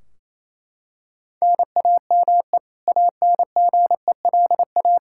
Morse code, heard as a keyed tone, about 28 wpm, 700 Hz. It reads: NAME ANGELA
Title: K1USN SST QSO Practice 28wpm